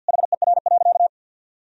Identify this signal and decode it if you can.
Morse code, heard as a keyed tone, about 53 wpm, 700 Hz. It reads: HER.
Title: Sentences from Top 1000 Words - Review Part 1 - 50wpm